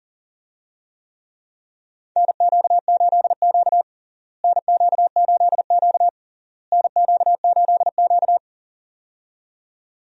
Morse code, heard as a keyed tone, about 40 wpm, 700 Hz. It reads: NQ8Q NQ8Q NQ8Q
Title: USA Call Signs - 4 Characters - Repeated 3x - 40wpm